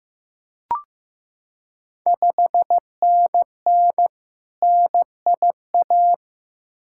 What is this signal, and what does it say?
Morse code, heard as a keyed tone, about 15 wpm, 690 Hz. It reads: 5NN NIA
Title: Ontario QSO Party Exchange 15wpm